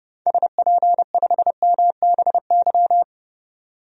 Morse code, heard as a keyed tone, about 32 wpm, 700 Hz. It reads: SP5MBY
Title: DXCC-100 Call Signs - 6 Characters - 30wpm